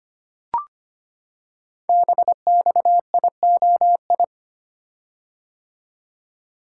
Morse code, heard as a keyed tone, about 25 wpm, 700 Hz. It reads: BXIOI
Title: Code Groups - 5 Letters - 25wpm